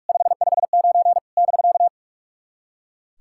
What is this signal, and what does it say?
Morse code, heard as a keyed tone, about 44 wpm, 690 Hz. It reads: 559 <BK>
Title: SOTA QSO Practice 45wpm